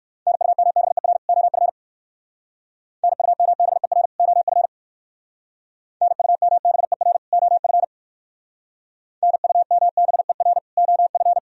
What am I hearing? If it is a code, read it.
Morse code, heard as a keyed tone, about 56 wpm, 700 Hz. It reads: NUMBER OF NUMBER OF NUMBER OF NUMBER OF
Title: Speed-Racing - Sets of 2 Words 45wpm